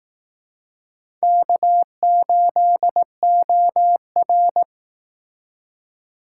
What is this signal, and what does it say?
Morse code, heard as a keyed tone, about 18 wpm, 700 Hz. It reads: K8OR
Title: Speed Racing - USA 1x2 Call Signs 15wpm